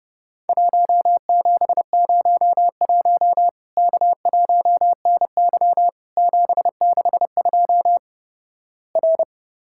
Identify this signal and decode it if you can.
Morse code, heard as a keyed tone, about 30 wpm, 700 Hz. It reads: 1701 K1NY 762 R
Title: Worked All Europe DX Contest - QTC Exchange - 30wpm